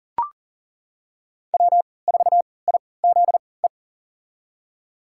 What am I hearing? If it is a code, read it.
Morse code, heard as a keyed tone, about 40 wpm, spaced as for 23 wpm, 700 Hz. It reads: W4IZE
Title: USA Call Signs - 5 Characters - 3x Character Spacing - 40wpm